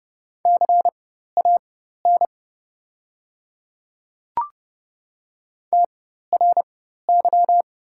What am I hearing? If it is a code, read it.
Morse code, heard as a keyed tone, about 30 wpm, spaced as for 14 wpm, 700 Hz. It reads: CAN TRY
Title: 4x Character Spacing - Top 200 Words - 30wpm